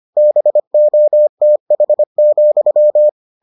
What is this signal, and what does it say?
Morse code, heard as a keyed tone, about 25 wpm, 600 Hz. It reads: BOTH,